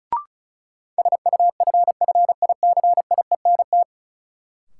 Morse code, heard as a keyed tone, about 36 wpm, 700 Hz. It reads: SUFFICIENT
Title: Sentences from Top 1400 Words - New Word Drill Alt - 35wpm